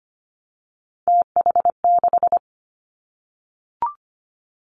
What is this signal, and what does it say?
Morse code, heard as a keyed tone, about 25 wpm, 700 Hz. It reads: TH6
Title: Three Challenging Letter-Number Combinations 25wpm